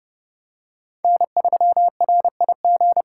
Morse code, heard as a keyed tone, about 26 wpm, 700 Hz. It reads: N3RIG